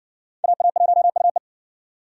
Morse code, heard as a keyed tone, about 58 wpm, 700 Hz. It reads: AA1UE